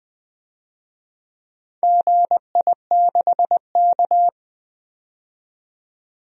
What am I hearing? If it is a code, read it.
Morse code, heard as a keyed tone, about 20 wpm, 700 Hz. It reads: GI6K
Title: DXCC-100 Call Signs - 4 Characters - 20wpm